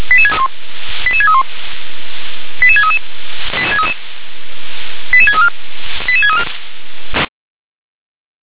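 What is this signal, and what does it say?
Five-tone selective call (ZVEI), heard as quick runs of five tones, tones of about 70 ms. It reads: 88631 88633 88634 88634
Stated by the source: ZVEI five-tone selective call received by SDR.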